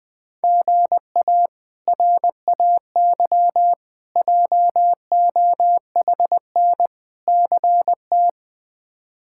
Morse code, heard as a keyed tone, about 20 wpm, 700 Hz. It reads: GA RAY JOHN CT
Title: K1USN SST QSO Practice - 20wpm